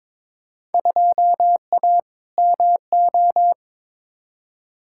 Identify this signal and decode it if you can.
Morse code, heard as a keyed tone, about 22 wpm, 700 Hz. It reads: 2A MO
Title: Field Day Exchanges Encore - 22wpm